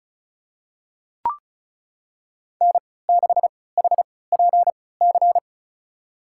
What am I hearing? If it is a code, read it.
Morse code, heard as a keyed tone, about 36 wpm, spaced as for 20 wpm, 700 Hz. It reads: N6HPC
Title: USA Call Signs - 5 Characters - 3x Character Spacing - 35wpm